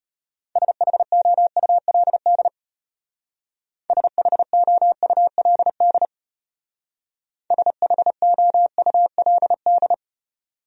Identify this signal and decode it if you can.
Morse code, heard as a keyed tone, about 32 wpm, 700 Hz. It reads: SHOULD SHOULD SHOULD